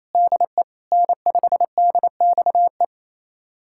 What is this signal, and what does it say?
Morse code, heard as a keyed tone, about 28 wpm, 700 Hz. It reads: DE N5DXE
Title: USA Callsign DE Callsign - 5 Characters - 28wpm